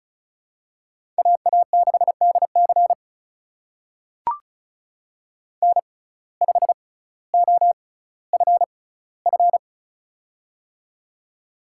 Morse code, heard as a keyed tone, about 34 wpm, spaced as for 18 wpm, 700 Hz. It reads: AA6DC N5OFF
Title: USA Call Signs - 6x Character Spacing - 35wpm